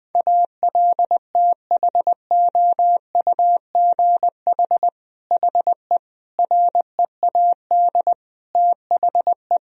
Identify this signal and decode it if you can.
Morse code, heard as a keyed tone, about 20 wpm, 700 Hz. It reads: ALTHOUGH HE READ THE
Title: Sentences from Top 1000 Words - Review Part 3 - 20wpm